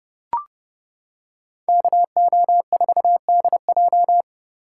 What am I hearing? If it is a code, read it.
Morse code, heard as a keyed tone, about 31 wpm, 700 Hz. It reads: KO4DJ